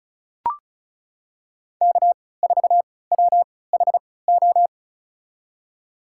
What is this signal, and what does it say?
Morse code, heard as a keyed tone, about 34 wpm, spaced as for 20 wpm, 700 Hz. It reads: K4WHO
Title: USA Call Signs - 5 Characters - 3x Character Spacing - 35wpm